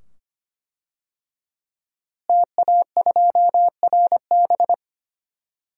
Morse code, heard as a keyed tone, about 25 wpm, 700 Hz. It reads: TA2RB
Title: DXCC-100 Call Signs - 5 Characters - Repeated 2x - 25wpm